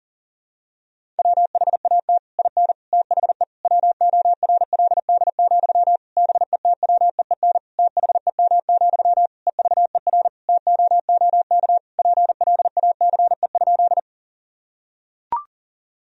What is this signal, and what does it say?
Morse code, heard as a keyed tone, about 40 wpm, 700 Hz. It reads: WHAT IN THE WORLD, BETWEEN THEM, EVER TOOK PLACE?
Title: Sentences from Top 1000 Words - Review Part 1 - 40wpm